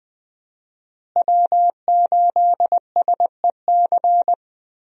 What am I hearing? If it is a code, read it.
Morse code, heard as a keyed tone, about 20 wpm, 700 Hz. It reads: W8SEC
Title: USA Call Signs - 5 Characters - Repeated 2x - 20wpm